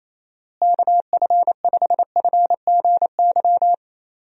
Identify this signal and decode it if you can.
Morse code, heard as a keyed tone, about 28 wpm, 700 Hz. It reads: KF5FGY
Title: USA Call Signs - 6 Characters - Repeated 2x - 28wpm